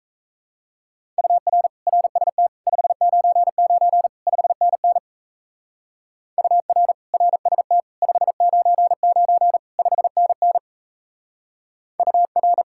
Morse code, heard as a keyed tone, about 34 wpm, 700 Hz. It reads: UR RST 599 5NN UR RST 599 5NN UR
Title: Speed-Racing Common QSO Phrases - 28wpm